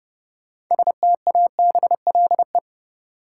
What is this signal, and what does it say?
Morse code, heard as a keyed tone, about 30 wpm, 700 Hz. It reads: STABLE
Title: Sentences from Top 1500 Words - New Word Drill Alt - 30wpm